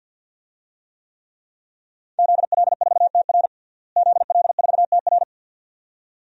Morse code, heard as a keyed tone, about 52 wpm, 700 Hz. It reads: ZL4TR ZL4TR
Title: DXCC-100 Call Signs - 5 Characters - Repeated 2x - 50wpm